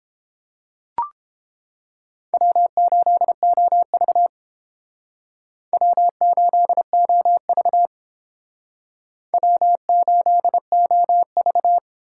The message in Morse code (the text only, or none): W8OV W8OV W8OV